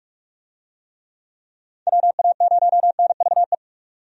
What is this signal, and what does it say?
Morse code, heard as a keyed tone, about 47 wpm, 700 Hz. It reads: WA0NVE
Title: USA Call Signs - 6 Characters - 45wpm